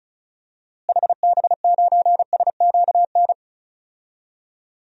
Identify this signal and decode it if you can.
Morse code, heard as a keyed tone, about 36 wpm, 700 Hz. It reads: HB9SQN
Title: DXCC-100 Call Signs - 6 Characters - Repeated 2x - 35wpm